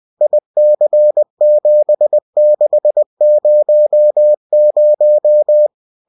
Morse code, heard as a keyed tone, about 20 wpm, 600 Hz. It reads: IC7600